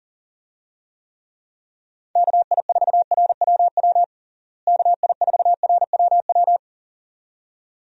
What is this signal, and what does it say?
Morse code, heard as a keyed tone, about 40 wpm, 700 Hz. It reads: KI4RWW KI4RWW
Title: USA Call Signs - 6 Characters - Repeated 2x - 40wpm